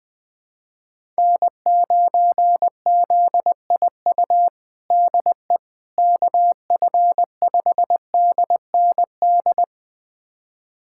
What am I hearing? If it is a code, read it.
Morse code, heard as a keyed tone, about 20 wpm, 700 Hz. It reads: N9ZIU DE KF5DND
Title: USA Callsign DE Callsign 20wpm